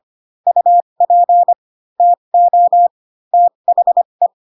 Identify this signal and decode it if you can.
Morse code, heard as a keyed tone, about 25 wpm, spaced as for 22 wpm, 700 Hz. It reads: UP TO THE